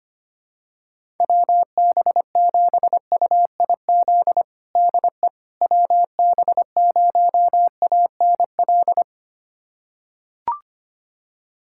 Morse code, heard as a keyed tone, about 25 wpm, 700 Hz. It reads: WB7UIZ DE WB0ANL
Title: USA Callsign DE Callsign - 6 Characters - 25wpm